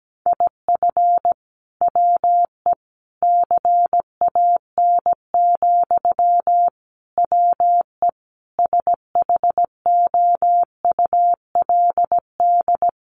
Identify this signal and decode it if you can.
Morse code, heard as a keyed tone, about 17 wpm, 700 Hz. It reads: IF WE CAN, WE SHOULD
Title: Sentences from Top 1100 Words 17wpm